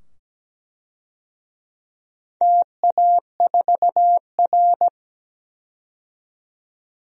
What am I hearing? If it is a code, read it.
Morse code, heard as a keyed tone, about 17 wpm, 700 Hz. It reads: TA4R
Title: DXCC-100 Call Signs - 4 Characters - 17wpm